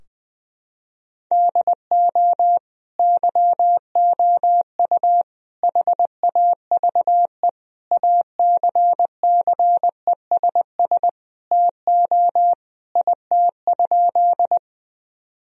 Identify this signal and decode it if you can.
Morse code, heard as a keyed tone, about 20 wpm, 700 Hz. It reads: DO YOU HAVE ACCESS TO IT?
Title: Sentences from Top 1000 Words - Review Part 4 - 20wpm